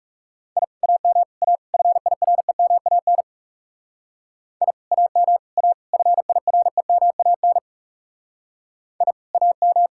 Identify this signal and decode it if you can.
Morse code, heard as a keyed tone, about 39 wpm, 700 Hz. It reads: I AM A FIREMAN I AM A FIREMAN I AM